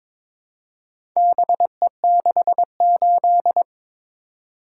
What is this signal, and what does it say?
Morse code, heard as a keyed tone, about 22 wpm, 700 Hz. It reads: BE68
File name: practice / 22wpm / Four Letter-Number Combinations - 22wpm